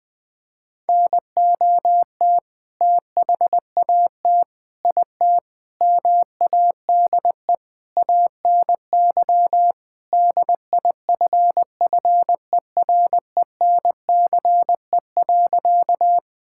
Morse code, heard as a keyed tone, about 20 wpm, 700 Hz. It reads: NOT THAT IT MADE ANY DIFFERENCE.
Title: Sentences from Top 1000 Words - Review Part 5 - 20wpm